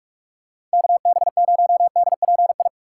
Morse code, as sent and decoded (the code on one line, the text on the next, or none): -.- -... ----- -.. .--. ..
KB0DPI